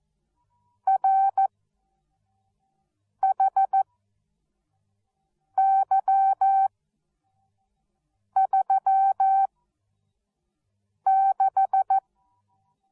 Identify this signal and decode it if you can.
Morse code, heard as a keyed tone, about 14 wpm, spaced as for 4 wpm, 770 Hz. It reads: RHY36